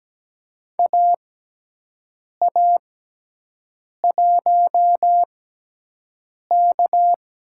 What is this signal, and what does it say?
Morse code, heard as a keyed tone, about 17 wpm, spaced as for 6 wpm, 700 Hz. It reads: AA1K
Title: USA Call Signs - 6x Character Spacing - 17wpm